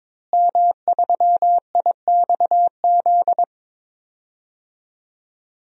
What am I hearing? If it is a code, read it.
Morse code, heard as a keyed tone, about 22 wpm, 700 Hz. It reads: M3IXZ
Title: DXCC-100 Call Signs - 5 Characters - Repeated 3x - 22wpm